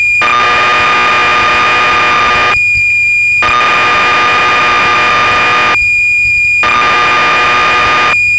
Cassette data (Kansas City type).